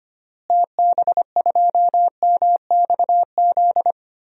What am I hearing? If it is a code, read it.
Morse code, heard as a keyed tone, about 25 wpm, 700 Hz. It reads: TB2MXZ
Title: DXCC-100 Call Signs - 6 Characters - Repeated 2x - 25wpm